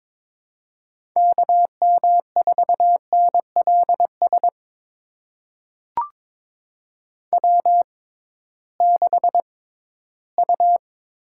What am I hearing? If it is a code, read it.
Morse code, heard as a keyed tone, about 22 wpm, spaced as for 14 wpm, 700 Hz. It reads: KM4NLS W6U
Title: USA Call Signs - 6x Character Spacing - 22wpm